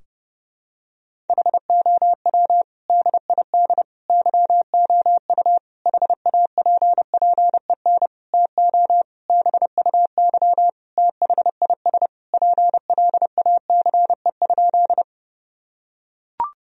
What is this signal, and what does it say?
Morse code, heard as a keyed tone, about 30 wpm, 700 Hz. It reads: HOW DID YOU HAPPEN TO BUY THIS PLACE?
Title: Sentences from Top 1000 Words - Review Part 4 - 30wpm